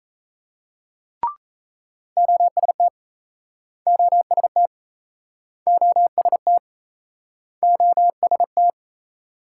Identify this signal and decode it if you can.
Morse code, heard as a keyed tone, about 37 wpm, 700 Hz. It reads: OST OST OST OST